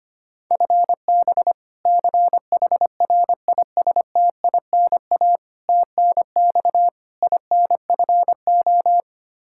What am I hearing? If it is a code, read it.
Morse code, heard as a keyed tone, about 25 wpm, 700 Hz. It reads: FB CHRISTINA TNX INFO
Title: Practice QSO - Exchange 3 - More Info - 25wpm